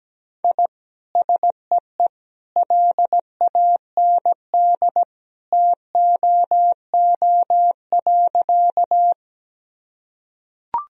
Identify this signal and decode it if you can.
Morse code, heard as a keyed tone, about 17 wpm, 700 Hz. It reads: I SEE LAND TOO.
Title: Sentences from Top 1000 Words - Review Part 1 - 17wpm